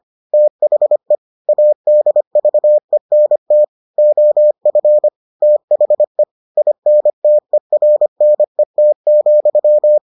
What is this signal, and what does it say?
Morse code, heard as a keyed tone, about 25 wpm, 600 Hz. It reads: THE ADVENT OF THE INTERNET,